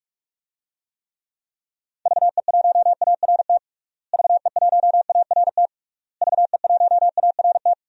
Morse code, heard as a keyed tone, about 47 wpm, 700 Hz. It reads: VE1ART VE1ART VE1ART
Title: DXCC-100 Call Signs - 6 Characters - Repeated 3x - 45wpm